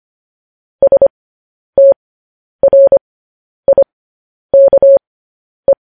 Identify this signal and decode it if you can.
Morse code, heard as a keyed tone, about 25 wpm, spaced as for 10 wpm, 550 Hz. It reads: STRIKE
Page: news-headlines-in-morse-code-at-10-wpm